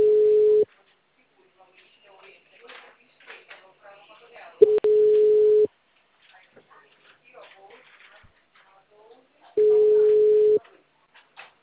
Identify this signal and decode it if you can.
Ringback tone (Europe, 425 Hz), with speech.
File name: out- -2012- - - .74810